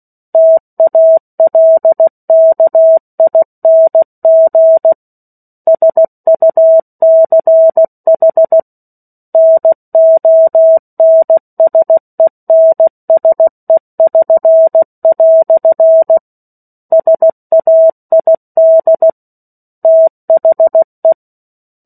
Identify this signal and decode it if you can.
Morse code, heard as a keyed tone, about 16 wpm, 650 Hz. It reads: TALKING SUCH NONSENSE<SN>" SAID THE
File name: 03-Chapter-03 16 WPM